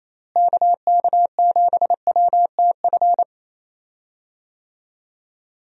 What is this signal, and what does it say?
Morse code, heard as a keyed tone, about 28 wpm, 700 Hz. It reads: KK7WTF
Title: USA Call Signs - 6 Characters - Repeated 2x - 28wpm